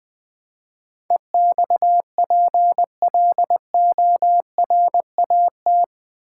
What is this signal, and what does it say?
Morse code, heard as a keyed tone, about 20 wpm, 700 Hz. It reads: EXPLORAT
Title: Build A Word - 4 Letters At A Time - 11-Letter Words - 20wpm